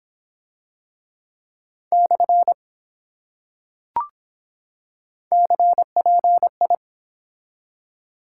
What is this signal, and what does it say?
Morse code, heard as a keyed tone, about 24 wpm, spaced as for 7 wpm, 700 Hz. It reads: /CPI